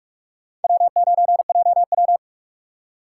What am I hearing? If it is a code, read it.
Morse code, heard as a keyed tone, about 43 wpm, 700 Hz. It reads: W9JW